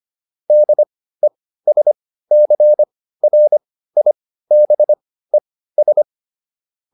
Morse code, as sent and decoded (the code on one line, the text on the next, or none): -.. ... -.-. .-. .. -... ...
DESCRIBES